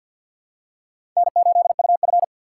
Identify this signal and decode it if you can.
Morse code, heard as a keyed tone, about 49 wpm, 700 Hz. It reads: N8UF
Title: USA Call Signs - 4 Characters - 2x Character Spacing - 50wpm